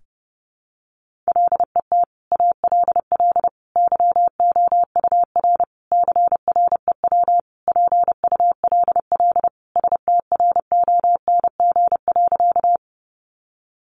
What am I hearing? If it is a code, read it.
Morse code, heard as a keyed tone, about 30 wpm, 700 Hz. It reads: LET ALL YOUR CREW PULL STRONG.
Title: Sentences from Top 1500 Words 30wpm